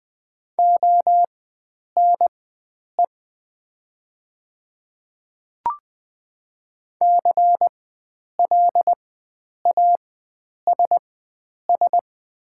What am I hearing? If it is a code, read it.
Morse code, heard as a keyed tone, about 20 wpm, spaced as for 9 wpm, 700 Hz. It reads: ONE CLASS
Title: 4x Character Spacing - Top 500 Words - 20wpm